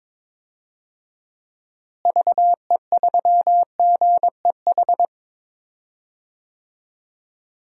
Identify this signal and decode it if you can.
Morse code, heard as a keyed tone, about 22 wpm, 700 Hz. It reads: VE3GEH
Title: DXCC-100 Call Signs - 6 Characters - Repeated 3x - 22wpm